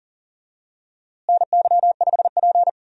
Morse code, heard as a keyed tone, about 37 wpm, 700 Hz. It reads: NY5P
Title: USA Call Signs - 4 Characters - Repeated 2x - 40wpm